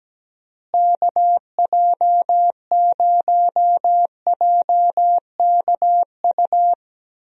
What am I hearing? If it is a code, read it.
Morse code, heard as a keyed tone, about 17 wpm, 700 Hz. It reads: KJ0JKU
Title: USA Call Signs - 6 Characters - Repeated 2x - 17wpm